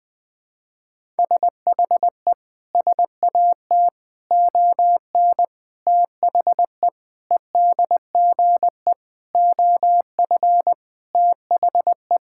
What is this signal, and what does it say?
Morse code, heard as a keyed tone, about 21 wpm, 700 Hz. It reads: SHE SAT ON THE EDGE OF THE
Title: Sentences with 4-Word Phrases - 20wpm